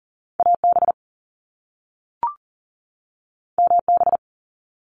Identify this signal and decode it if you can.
Morse code, heard as a keyed tone, about 40 wpm, 700 Hz. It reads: AB MB
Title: Canadian Provinces and Territories 40wpm